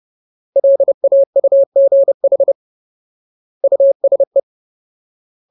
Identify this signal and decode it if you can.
Morse code, heard as a keyed tone, about 30 wpm, 550 Hz. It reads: LAUGH USE
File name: Ultimate Word List / 30WPM Ultimate